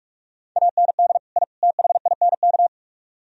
Morse code, heard as a keyed tone, about 59 wpm, spaced as for 51 wpm, 700 Hz. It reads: AND I THINK
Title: Speed-Racing - Sets of 3 Words 45wpm